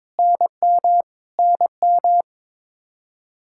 Morse code, heard as a keyed tone, about 22 wpm, 700 Hz. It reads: NM NM
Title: Practice QSO - Exchange 1 - Beginning - 22wpm